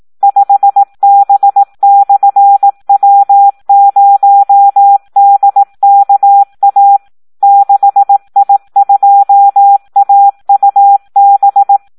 Morse code, heard as a keyed tone, about 18 wpm, 800 Hz. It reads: 5B/W0DKA 6I2AUB
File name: Audio practica CW / SD 1 18